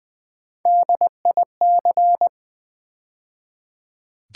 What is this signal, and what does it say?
Morse code, heard as a keyed tone, about 20 wpm, 700 Hz. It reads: DIC
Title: Build A Word - 3 Letters At A Time - 8-Letter Words - 20wpm